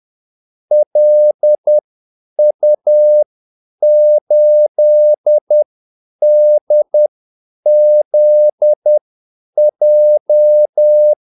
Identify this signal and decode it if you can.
Morse code, heard as a keyed tone, about 10 wpm, spaced as for 8 wpm, 600 Hz. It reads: LU8DZJ
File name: Callsigns-10-8wpm